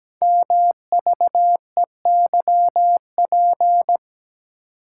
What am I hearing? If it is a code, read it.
Morse code, heard as a keyed tone, about 17 wpm, 700 Hz. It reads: MVEYP
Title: Code Groups - 5 Letters - 17wpm